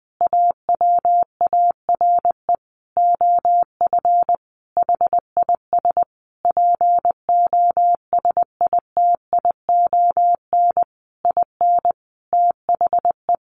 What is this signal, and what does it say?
Morse code, heard as a keyed tone, about 19 wpm, 700 Hz. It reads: AWARE OF HIS POSITION IN THE
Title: Sentences from Top 1200 Words 20wpm